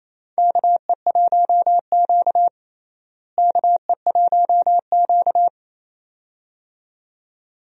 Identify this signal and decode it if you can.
Morse code, heard as a keyed tone, about 28 wpm, 700 Hz. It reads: KE1Q KE1Q
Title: USA Call Signs - 4 Characters - Repeated 2x - 28wpm